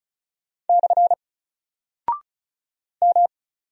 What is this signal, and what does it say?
Morse code, heard as a keyed tone, about 36 wpm, 700 Hz. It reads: / M
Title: Single Character (Letter:Number:Punctuation:Prosign) Alt 35wpm